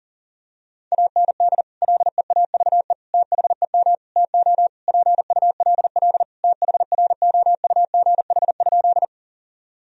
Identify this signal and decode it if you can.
Morse code, heard as a keyed tone, about 37 wpm, 700 Hz. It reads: AND LEAVE THEM TO PULL THROUGH?
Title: Sentences from Top 1000 Words - Review Part 2 - 40wpm